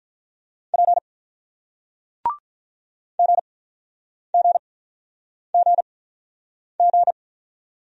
Morse code, heard as a keyed tone, about 48 wpm, 700 Hz. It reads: P G G G G